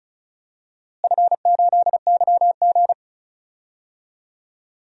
Morse code, heard as a keyed tone, about 34 wpm, 700 Hz. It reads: F8YG
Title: DXCC-100 Call Signs - 4 Characters - Repeated 3x - 35wpm